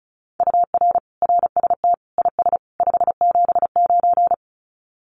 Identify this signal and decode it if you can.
Morse code, heard as a keyed tone, about 34 wpm, 700 Hz. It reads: UR RST IS 579
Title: Common QSO Phrases 35wpm